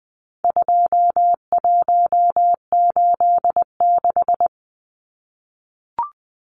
Morse code, heard as a keyed tone, about 20 wpm, 700 Hz. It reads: 2186
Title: Numbers - 4-Digit 20wpm